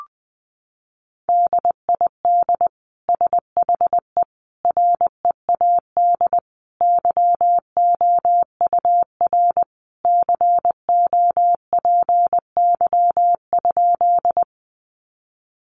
Morse code, heard as a keyed tone, about 20 wpm, 700 Hz. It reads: DID SHE READ YOUR COPY?
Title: Sentences from Top 1400 Words 20wpm